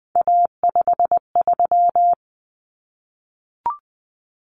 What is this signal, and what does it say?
Morse code, heard as a keyed tone, about 20 wpm, 700 Hz. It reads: A53
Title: Common Cut Numbers - 3-Digit 20wpm